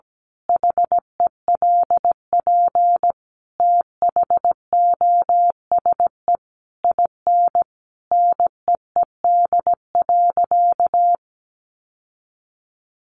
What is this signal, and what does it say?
Morse code, heard as a keyed tone, about 17 wpm, 700 Hz. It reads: HELP THOSE IN NEED.
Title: Sentences from Top 1200 Words 17wpm